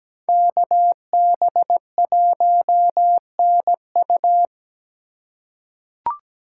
Morse code, heard as a keyed tone, about 17 wpm, 700 Hz. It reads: KB1NU
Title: USA Call Signs - 3x Character Spacing - 17wpm